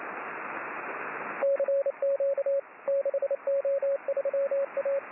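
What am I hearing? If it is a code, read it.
Morse code, heard as a keyed tone, about 28 wpm, 550 Hz, with radio noise. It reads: CQ 6O3A
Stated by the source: Morse signal on the 6 m band.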